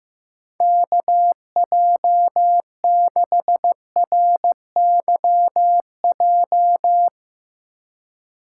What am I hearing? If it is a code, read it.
Morse code, heard as a keyed tone, about 15 wpm, 700 Hz. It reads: KJ6RYJ
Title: Speed Racing - USA 2x3 Call Signs 15wpm